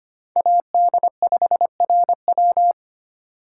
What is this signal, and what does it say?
Morse code, heard as a keyed tone, about 25 wpm, 700 Hz. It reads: AD5RW
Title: USA Call Signs - 5 Characters - 3x Character Spacing - 25wpm